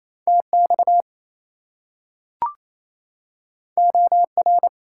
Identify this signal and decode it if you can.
Morse code, heard as a keyed tone, about 28 wpm, 700 Hz. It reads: TX OR
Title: US States and Canadian Provinces 28wpm